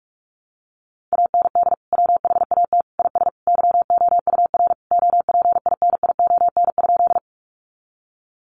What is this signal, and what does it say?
Morse code, heard as a keyed tone, about 44 wpm, 700 Hz. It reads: AND WHAT IS YOUR OPINION?
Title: Sentences from Top 1200 Words 45wpm